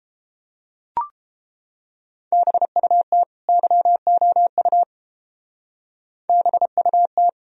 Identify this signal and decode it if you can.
Morse code, heard as a keyed tone, about 29 wpm, 700 Hz. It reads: BUT YOU BUT